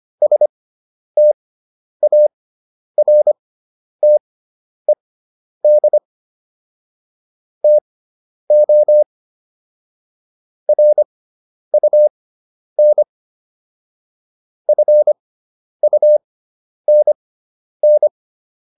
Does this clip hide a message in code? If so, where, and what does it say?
Yes: Morse code from 0.2 to 18.1 s, heard as a keyed tone, about 25 wpm, spaced as for 10 wpm, 600 Hz. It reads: STARTED TO RUN FUNN